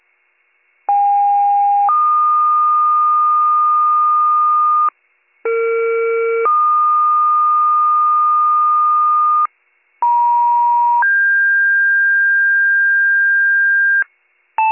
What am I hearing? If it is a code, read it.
Two-tone page, heard as a one-second tone, then a three-second tone, about 810 then 1220 Hz, about 460 then 1220 Hz, about 950 then 1600 Hz. It is heard from 0.9 to 14.0 s.